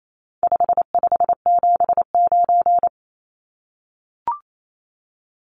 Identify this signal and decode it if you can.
Morse code, heard as a keyed tone, about 28 wpm, 700 Hz. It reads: 5579
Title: Numbers - 4-Digit 28wpm